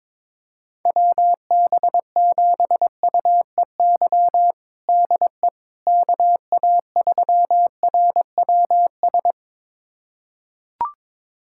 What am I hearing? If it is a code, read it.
Morse code, heard as a keyed tone, about 21 wpm, 700 Hz. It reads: WB7UEY DE KA3RWS
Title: USA Callsign DE Callsign - 6 Characters - 22wpm